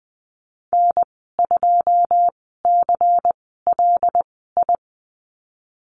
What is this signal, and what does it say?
Morse code, heard as a keyed tone, about 20 wpm, spaced as for 15 wpm, 700 Hz. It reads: N2CLI
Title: 154 Lesson 25g - Letter p - Call Signs 20wpm